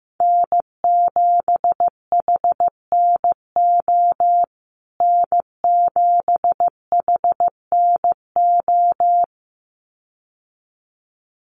Morse code, heard as a keyed tone, about 15 wpm, 700 Hz. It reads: N7HNO N7HNO
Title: USA Call Signs Repeated 2x 15wpm